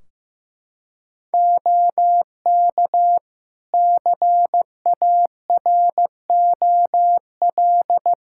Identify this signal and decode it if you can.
Morse code, heard as a keyed tone, about 15 wpm, 700 Hz. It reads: OK CAROL
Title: K1USN SST QSO Practice 15wpm